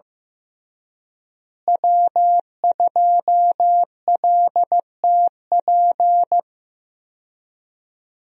Morse code, heard as a keyed tone, about 15 wpm, 700 Hz. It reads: W2LTP
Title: USA Call Signs - 5 Characters - Repeated 3x - 15wpm